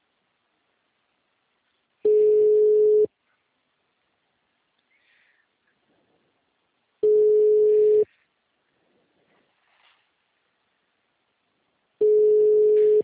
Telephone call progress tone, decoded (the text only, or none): ringback tone